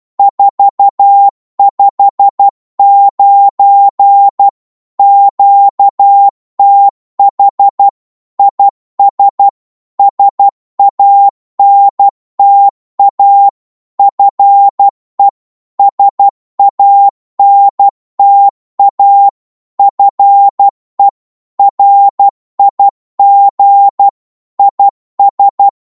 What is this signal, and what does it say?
Morse code, heard as a keyed tone, about 12 wpm, 800 Hz. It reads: 459 QTH IS SANTA FE SANTA FE RIG IS
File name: Morse / qso-12wpm